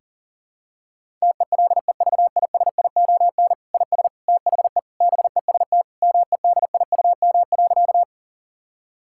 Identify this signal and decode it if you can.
Morse code, heard as a keyed tone, about 40 wpm, 700 Hz. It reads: TELEVISION IS THE BEST MEDIUM.
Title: Sentences from Top 1000 Words - Review Part 6 - 40wpm